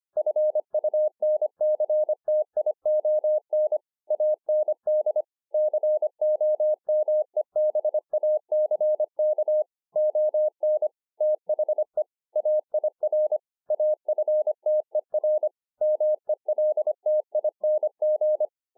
Morse code, heard as a keyed tone, about 24 wpm, 610 Hz. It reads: FUNCTION AND COMEBACK ON THE AIR AFTER MELTING